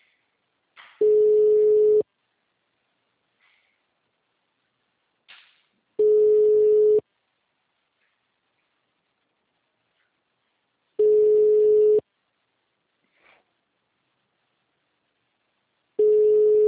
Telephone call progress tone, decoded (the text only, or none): ringback tone